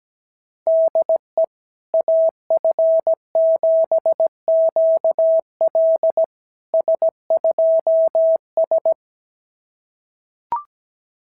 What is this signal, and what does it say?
Morse code, heard as a keyed tone, about 17 wpm, 690 Hz. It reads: DE AF7QL S2S
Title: SOTA QSO Practice 17wpm